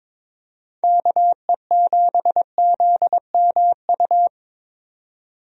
Morse code, heard as a keyed tone, about 21 wpm, 700 Hz. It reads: KE7ZMU
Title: USA Call Signs - 6 Characters - 22wpm